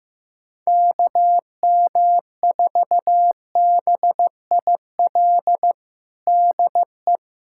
Morse code, heard as a keyed tone, about 15 wpm, 700 Hz. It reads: KM4BIL DE
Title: USA Callsign DE Callsign - 6 Characters - 15wpm